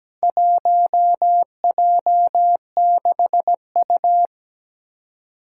Morse code, heard as a keyed tone, about 17 wpm, 700 Hz. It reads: 1J6U
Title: Four Challenging Letter-Number Combinations 17wpm